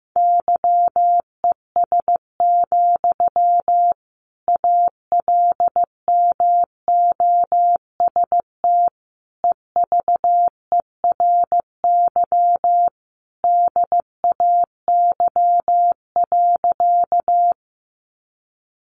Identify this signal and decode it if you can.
Morse code, heard as a keyed tone, about 15 wpm, 700 Hz. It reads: YES, ALMOST EVERY DAY.
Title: Sentences from Top 700 Words 15wpm